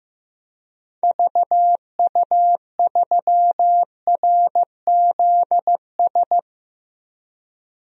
Morse code, heard as a keyed tone, about 15 wpm, 700 Hz. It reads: VU3RZS
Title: DXCC-100 Call Signs - 6 Characters - 15wpm